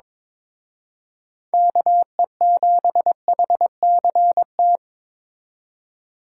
Morse code, heard as a keyed tone, about 26 wpm, 700 Hz. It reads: KE7HCT
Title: Speed Racing - USA 2x3 Call Signs 22wpm